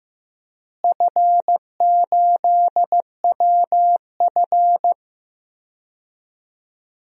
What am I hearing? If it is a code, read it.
Morse code, heard as a keyed tone, about 15 wpm, 700 Hz. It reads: F8WF
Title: DXCC-100 Call Signs - 4 Characters - Repeated 2x - 15wpm